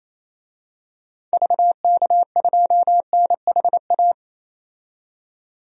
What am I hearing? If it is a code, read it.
Morse code, heard as a keyed tone, about 28 wpm, 700 Hz. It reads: VK2NHA
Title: DXCC-100 Call Signs - 6 Characters - 28wpm